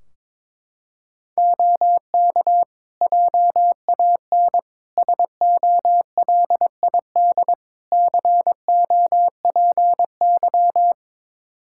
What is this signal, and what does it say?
Morse code, heard as a keyed tone, about 22 wpm, 700 Hz. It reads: OK JAN SOLID COPY
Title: K1USN SST QSO Practice 22wpm